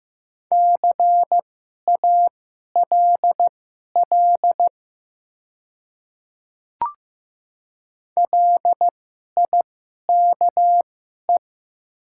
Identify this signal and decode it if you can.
Morse code, heard as a keyed tone, about 15 wpm, spaced as for 11 wpm, 700 Hz. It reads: CALL LIKE
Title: 2x Character Spacing - Top 200 Words - 15wpm